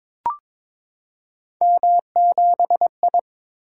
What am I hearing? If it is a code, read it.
Morse code, heard as a keyed tone, about 21 wpm, 700 Hz. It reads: M7I
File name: practice / 22wpm / Three Letter-Number Combinations - 22wpm